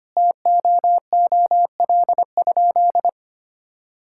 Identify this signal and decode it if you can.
Morse code, heard as a keyed tone, about 25 wpm, 700 Hz. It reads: TOOL?
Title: Sentences from Top 1000 Words - Review Part 6 - 25wpm